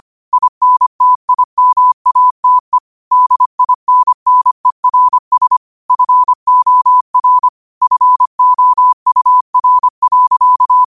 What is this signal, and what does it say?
Morse code, heard as a keyed tone, about 25 wpm, 1000 Hz. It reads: INTIMATE DINNERS FOR FOUR.
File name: qotdfiles / 25wpm